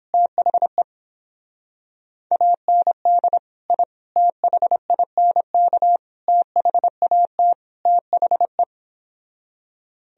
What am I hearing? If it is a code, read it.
Morse code, heard as a keyed tone, about 29 wpm, 700 Hz. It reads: THE AND I THINK THAT THE